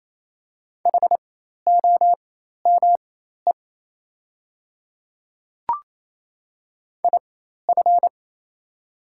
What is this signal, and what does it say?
Morse code, heard as a keyed tone, about 28 wpm, spaced as for 13 wpm, 700 Hz. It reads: HOME IF
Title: 4x Character Spacing - Top 200 Words - 28wpm